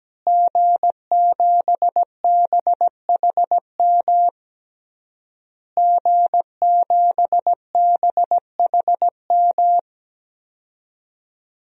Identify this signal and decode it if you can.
Morse code, heard as a keyed tone, about 17 wpm, 700 Hz. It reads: G7BHM G7BHM
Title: DXCC-100 Call Signs - 5 Characters - Repeated 3x - 17wpm